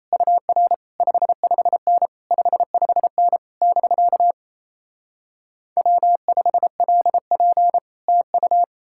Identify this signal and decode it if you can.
Morse code, heard as a keyed tone, about 37 wpm, 700 Hz. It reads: UR 55N 55N <BK> W5LP TU